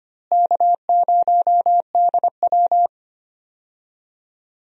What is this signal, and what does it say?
Morse code, heard as a keyed tone, about 25 wpm, 700 Hz. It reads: K0DW
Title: USA Call Signs - 4 Characters - Repeated 3x - 25wpm